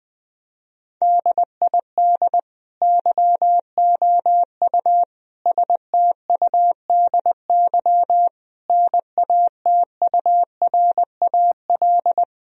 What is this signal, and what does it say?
Morse code, heard as a keyed tone, about 20 wpm, 700 Hz. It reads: DID YOU STUDY NATURAL
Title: Sentences from Top 1000 Words - Review Part 4 - 20wpm